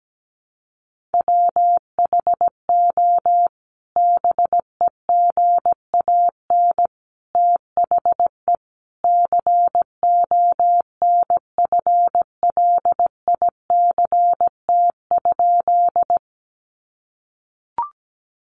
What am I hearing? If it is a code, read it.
Morse code, heard as a keyed tone, about 17 wpm, 700 Hz. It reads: WHO BEGAN THE CONFLICT?
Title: Sentences from Top 1300 Words 17wpm